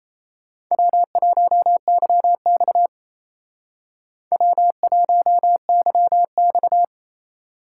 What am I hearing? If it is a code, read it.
Morse code, heard as a keyed tone, about 31 wpm, 700 Hz. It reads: W1YX W1YX
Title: Speed Racing - USA 1x2 Call Signs 28wpm